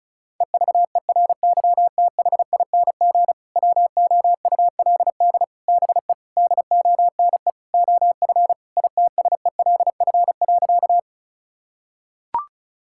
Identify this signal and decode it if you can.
Morse code, heard as a keyed tone, about 36 wpm, 700 Hz. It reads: EVERYTHING WOULD BE DONE OF ITSELF.
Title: Sentences from Top 1000 Words - Review Part 5 - 35wpm